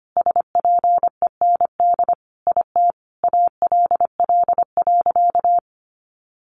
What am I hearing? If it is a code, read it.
Morse code, heard as a keyed tone, about 25 wpm, 700 Hz. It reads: SPEND IT ALL.
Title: Sentences from Top 800 Words 25wpm